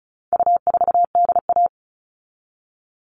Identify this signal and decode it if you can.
Morse code, heard as a keyed tone, about 34 wpm, 700 Hz. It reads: U4DA
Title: All Cut Numbers - 4-Digit 35wpm